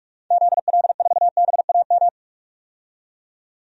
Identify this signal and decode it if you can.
Morse code, heard as a keyed tone, about 42 wpm, 700 Hz. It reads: ZL4BAM
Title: DXCC-100 Call Signs - 6 Characters - 45wpm